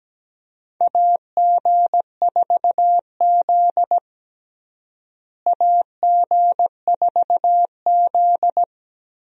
Morse code, heard as a keyed tone, about 17 wpm, 700 Hz. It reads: AG4Z AG4Z
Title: USA Call Signs - 4 Characters - Repeated 2x - 17wpm